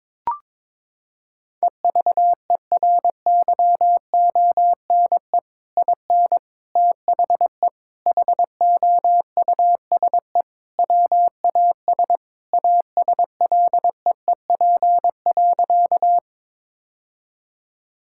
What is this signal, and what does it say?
Morse code, heard as a keyed tone, about 22 wpm, 700 Hz. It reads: EVERYONE IN THE HOUSE WAS ASLEEP.
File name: practice / 22wpm / Sentences with 3-Word Phrases - 22wpm